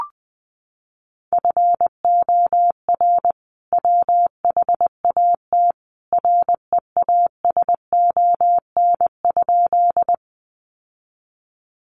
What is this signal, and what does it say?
Morse code, heard as a keyed tone, about 19 wpm, 700 Hz. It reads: FOR WHAT REASON?
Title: Sentences from Top 700 Words 20wpm